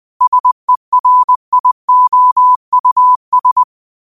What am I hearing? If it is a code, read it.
Morse code, heard as a keyed tone, about 20 wpm, 1000 Hz. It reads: SERIOUS
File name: qotdfiles / 20wpm